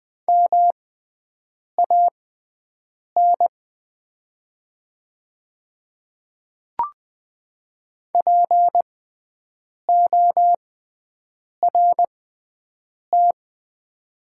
Morse code, heard as a keyed tone, about 20 wpm, spaced as for 7 wpm, 700 Hz. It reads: MAN PORT
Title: 6x Character Spacing - Top 500 Words - 20wpm